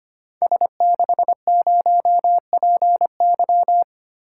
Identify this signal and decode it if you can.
Morse code, heard as a keyed tone, about 25 wpm, 700 Hz. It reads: S60PY
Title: Code Groups - 5 Letter:Numbers - 25wpm